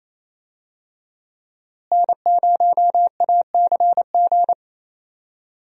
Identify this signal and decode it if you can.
Morse code, heard as a keyed tone, about 28 wpm, 700 Hz. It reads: N0ACG
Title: USA Call Signs - 5 Characters - Repeated 3x - 28wpm